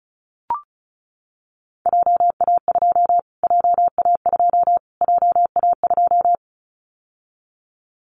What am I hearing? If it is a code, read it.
Morse code, heard as a keyed tone, about 34 wpm, 700 Hz. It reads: JA2 JA2 JA2
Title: Callsign Prefixes for Top 25 DXCC - Repeated 3x - 35wpm